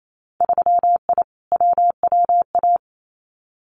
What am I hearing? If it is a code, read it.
Morse code, heard as a keyed tone, about 28 wpm, 700 Hz. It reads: 3I WWA
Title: Winter Field Day Exchanges Encore 28wpm